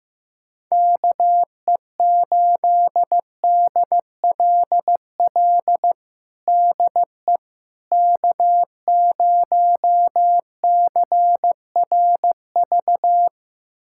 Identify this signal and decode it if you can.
Morse code, heard as a keyed tone, about 15 wpm, 700 Hz. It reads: KE8DLL DE K0CRV
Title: USA Callsign DE Callsign 15wpm